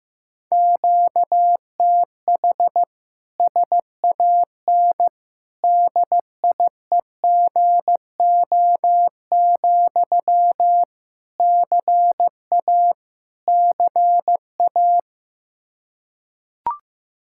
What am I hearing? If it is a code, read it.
Morse code, heard as a keyed tone, about 15 wpm, 700 Hz. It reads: QTH SAN DIEGO, CA CA
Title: Practice QSO - Exchange 1 - Beginning - 15wpm